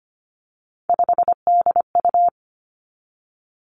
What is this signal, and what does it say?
Morse code, heard as a keyed tone, about 25 wpm, 700 Hz. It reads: 5DU
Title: Three Challenging Letter-Number Combinations 25wpm